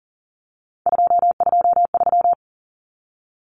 Morse code, heard as a keyed tone, about 40 wpm, 700 Hz. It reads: 223
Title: Common Cut Numbers - 3-Digit 40wpm